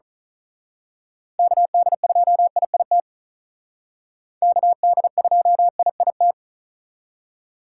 Morse code, heard as a keyed tone, about 39 wpm, 700 Hz. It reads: KD2IIT KD2IIT
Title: Speed Racing - USA 2x3 Call Signs 35wpm